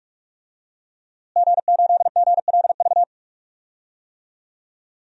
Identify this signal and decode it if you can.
Morse code, heard as a keyed tone, about 43 wpm, 700 Hz. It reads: G8GLV
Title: DXCC-100 Call Signs - 5 Characters - Repeated 3x - 45wpm